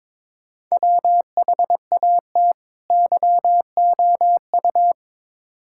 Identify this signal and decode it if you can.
Morse code, heard as a keyed tone, about 22 wpm, 700 Hz. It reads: WHAT YOU
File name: Build A Phrase - 2-Word Phrases - 22wpm